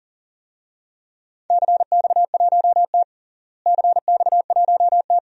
Morse code, heard as a keyed tone, about 41 wpm, 700 Hz. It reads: CX1T CX1T
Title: DXCC-100 Call Signs - 4 Characters - Repeated 2x - 40wpm